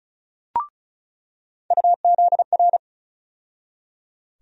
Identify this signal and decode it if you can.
Morse code, heard as a keyed tone, about 35 wpm, 700 Hz. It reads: UZR
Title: USA Call Sign Postfixes - 3 Letters 35wpm